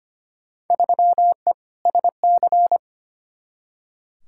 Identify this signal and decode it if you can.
Morse code, heard as a keyed tone, about 25 wpm, 700 Hz. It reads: 3E SC
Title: Field Day Exchanges Encore - 25wpm